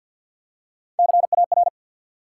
Morse code, heard as a keyed tone, about 50 wpm, 700 Hz. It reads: CAR
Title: Three Letter Words - 50wpm